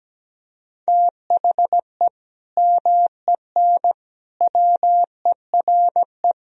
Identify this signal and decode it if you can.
Morse code, heard as a keyed tone, about 17 wpm, 700 Hz. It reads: THE MEN WERE
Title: Sentences from Top 1000 Words - Review Part 3 - 17wpm